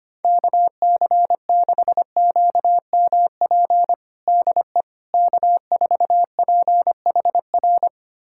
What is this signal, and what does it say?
Morse code, heard as a keyed tone, about 25 wpm, 700 Hz. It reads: KC6QMP DE K4PHR
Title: USA Callsign DE Callsign 25wpm